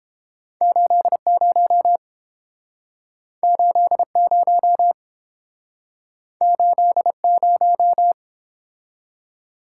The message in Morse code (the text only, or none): 80 80 80